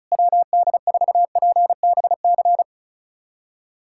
Morse code, heard as a keyed tone, about 34 wpm, 700 Hz. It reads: WD4PBC
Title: USA Callsign DE Callsign - 6 Characters - 35wpm